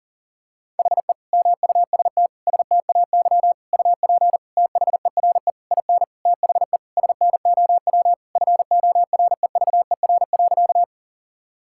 Morse code, heard as a keyed tone, about 40 wpm, 700 Hz. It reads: HE MUST STAY UP THERE IN THE SNOW FOREVER.
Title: Sentences from Top 1600 Words 40wpm